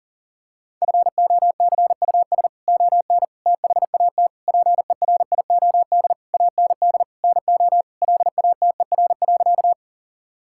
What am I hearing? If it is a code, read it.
Morse code, heard as a keyed tone, about 40 wpm, 700 Hz. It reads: FOCUS ON THAT PERIOD AND NO LATER.
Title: Sentences from Top 1000 Words - Review Part 5 - 40wpm